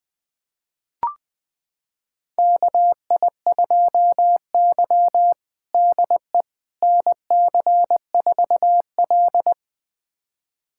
Morse code, heard as a keyed tone, about 20 wpm, 700 Hz. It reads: KI2Y DE NC4L
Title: USA Callsign DE Callsign - 4 Characters - 20wpm